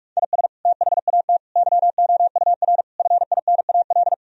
Morse code, heard as a keyed tone, about 46 wpm, 700 Hz. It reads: IS THAT YOUR FINAL
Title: Sentences from Top 1000 Words - Review Part 2 - 45wpm